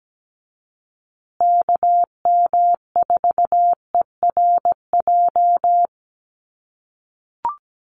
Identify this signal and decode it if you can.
Morse code, heard as a keyed tone, about 17 wpm, 700 Hz. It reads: KM4ERJ
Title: USA Call Signs Repeated 2x 17wpm